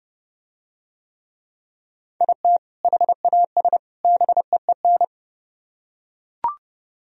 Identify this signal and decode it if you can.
Morse code, heard as a keyed tone, about 30 wpm, 700 Hz. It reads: IT HAS BEEN